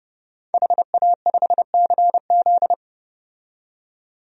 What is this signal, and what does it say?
Morse code, heard as a keyed tone, about 30 wpm, 700 Hz. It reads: HA5CZ
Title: DXCC-100 Call Signs - 5 Characters - 30wpm